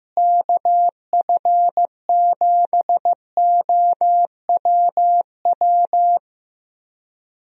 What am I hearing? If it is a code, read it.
Morse code, heard as a keyed tone, about 15 wpm, 700 Hz. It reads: KF7OWW
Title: USA Call Signs - 6 Characters - Repeated 3x - 15wpm